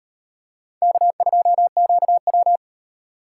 Morse code, heard as a keyed tone, about 37 wpm, 700 Hz. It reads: K2QW